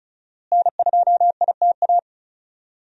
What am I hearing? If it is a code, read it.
Morse code, heard as a keyed tone, about 36 wpm, 700 Hz. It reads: N2ITA
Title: USA Call Signs - 5 Characters - Repeated 3x - 35wpm